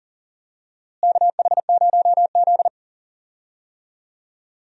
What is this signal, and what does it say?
Morse code, heard as a keyed tone, about 39 wpm, 700 Hz. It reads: KH0Z
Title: USA Call Signs - 4 Characters - Repeated 2x - 40wpm